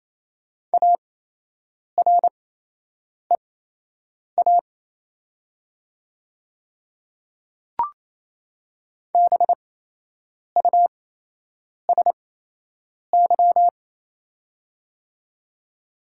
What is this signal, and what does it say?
Morse code, heard as a keyed tone, about 28 wpm, spaced as for 8 wpm, 690 Hz. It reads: AREA BUSY
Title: 8x Character Spacing - Top 500 Words - 28wpm